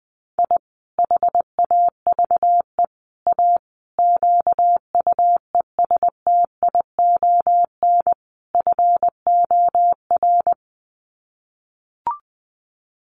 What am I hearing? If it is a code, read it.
Morse code, heard as a keyed tone, about 20 wpm, 700 Hz. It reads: I HAVE A QUESTION FOR
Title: Sets of 5 Words Encore2 20wpm